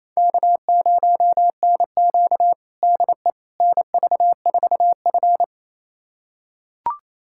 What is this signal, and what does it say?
Morse code, heard as a keyed tone, about 28 wpm, 700 Hz. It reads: K0NQ DE NV4F
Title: USA Callsign DE Callsign - 4 Characters - 28wpm